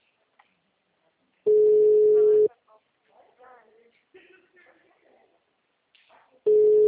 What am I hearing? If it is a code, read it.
Ringback tone (Europe, 425 Hz), with speech.